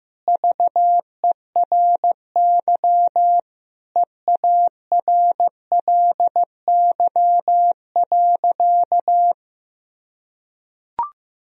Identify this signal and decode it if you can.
Morse code, heard as a keyed tone, about 15 wpm, 700 Hz. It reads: VERY EARLY.
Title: Sentences from Top 1000 Words - Review Part 2 - 15wpm